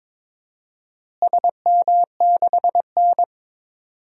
Morse code, heard as a keyed tone, about 22 wpm, 700 Hz. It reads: SM6N
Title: DXCC-100 Call Signs - 4 Characters - Repeated 3x - 22wpm